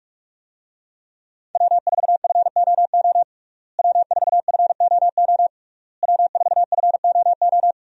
Morse code, heard as a keyed tone, about 43 wpm, 700 Hz. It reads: W4FOO W4FOO W4FOO
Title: USA Call Signs - 5 Characters - Repeated 3x - 45wpm